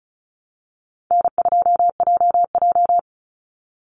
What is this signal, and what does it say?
Morse code, heard as a keyed tone, about 34 wpm, 700 Hz. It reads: N2JJ
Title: Easy Call Signs Repeated 2x - 35wpm